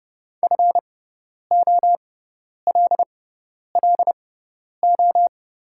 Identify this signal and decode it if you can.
Morse code, heard as a keyed tone, about 30 wpm, spaced as for 10 wpm, 690 Hz. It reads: FOLLO
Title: 6x Character Spacing - Top 500 Words - 30wpm